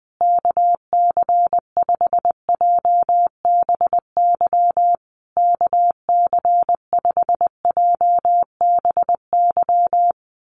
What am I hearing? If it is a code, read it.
Morse code, heard as a keyed tone, about 20 wpm, 700 Hz. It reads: KC5JBY KC5JBY
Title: USA Call Signs Repeated 3x 20wpm